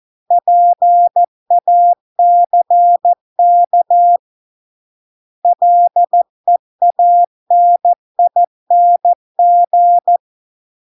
Morse code, heard as a keyed tone, about 14 wpm, 700 Hz. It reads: PACK LEANING